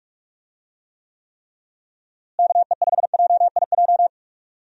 Morse code, heard as a keyed tone, about 41 wpm, 700 Hz. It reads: KE5JIJ
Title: USA Call Signs - 6 Characters - Repeated 3x - 45wpm